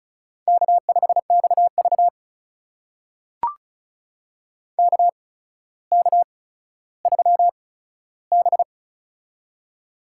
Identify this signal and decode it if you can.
Morse code, heard as a keyed tone, about 36 wpm, spaced as for 15 wpm, 700 Hz. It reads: K5XV KK3B
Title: USA Call Signs - 4 Characters - 8x Character Spacing - 35wpm